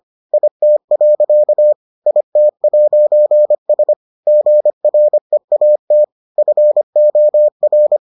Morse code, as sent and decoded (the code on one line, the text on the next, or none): .. - .-.-.- / .. - .----. ... / --. .-. .- - / ..-. --- .-.
IT. IT'S GREAT FOR